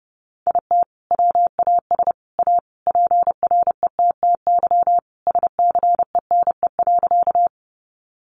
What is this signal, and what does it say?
Morse code, heard as a keyed tone, about 30 wpm, 700 Hz. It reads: IT WAS A PRETTY SCENE.
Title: Sentences from Top 1100 Words 30wpm